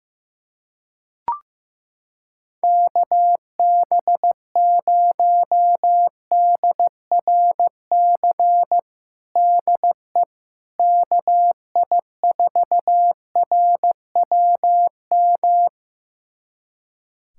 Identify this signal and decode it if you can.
Morse code, heard as a keyed tone, about 15 wpm, 700 Hz. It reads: KB0DRC DE KI4RWM
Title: USA Callsign DE Callsign - 6 Characters - 15wpm